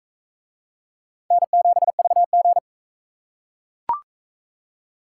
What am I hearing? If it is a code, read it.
Morse code, heard as a keyed tone, about 42 wpm, 700 Hz. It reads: N7VG